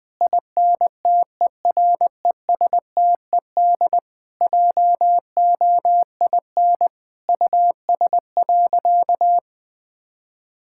Morse code, heard as a keyed tone, about 20 wpm, 700 Hz. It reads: INTERESTED JOIN US.
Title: Sentences with 6-Word Phrases - Spoken Beginning - 20wpm